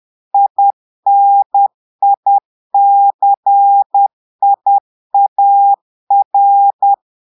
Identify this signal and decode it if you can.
Morse code, heard as a keyed tone, about 10 wpm, 800 Hz. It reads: INICIAR
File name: portugais texte en clair 2 vitesse 10 mots minutes